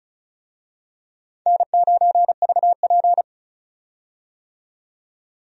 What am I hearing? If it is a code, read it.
Morse code, heard as a keyed tone, about 34 wpm, 700 Hz. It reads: N9VP
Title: USA Call Signs - 4 Characters - 35wpm